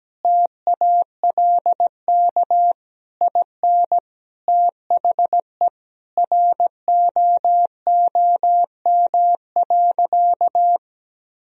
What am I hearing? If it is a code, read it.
Morse code, heard as a keyed tone, about 17 wpm, 700 Hz. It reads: TALK IN THE ROOM.
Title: Sentences from Top 1000 Words - Review Part 4 - 17wpm